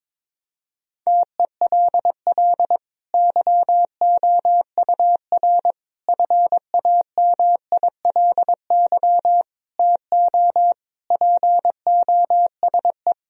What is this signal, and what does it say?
Morse code, heard as a keyed tone, about 22 wpm, 700 Hz. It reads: TELL YOUR FAMILY TO POSE
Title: Sentences from Top 1000 Words - Review Part 2 - 22wpm